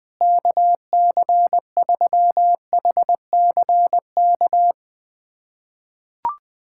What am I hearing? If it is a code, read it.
Morse code, heard as a keyed tone, about 20 wpm, 700 Hz. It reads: KC3HCK
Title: USA Callsign DE Callsign - 6 Characters - 20wpm